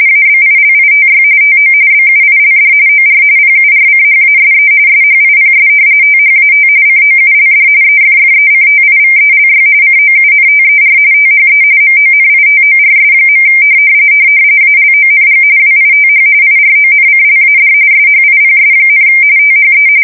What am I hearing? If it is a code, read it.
RTTY, about 45.45 baud. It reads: ABILITY TO BUILD OR PURCHASE SHIPS THAT COULD EFFECTIVELY COUNTER THE MONITORS. ALONG WITH IRONCLAD SHIPS, THE
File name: fsk170 / usnavy3